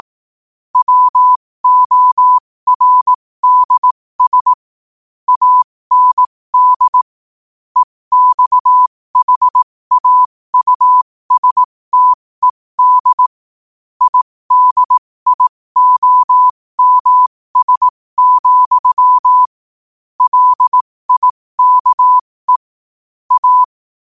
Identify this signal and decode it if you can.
Morse code, heard as a keyed tone, about 18 wpm, spaced as for 15 wpm, 1000 Hz. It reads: WORDS AND EXHAUSTED IDIOMS, LIKE A
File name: qotdfiles / 15wpm